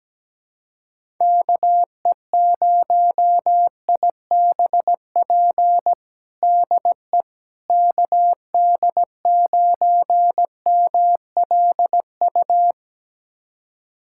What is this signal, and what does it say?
Morse code, heard as a keyed tone, about 17 wpm, 700 Hz. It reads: KE0IBP DE KD9MLU
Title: USA Callsign DE Callsign - 6 Characters - 17wpm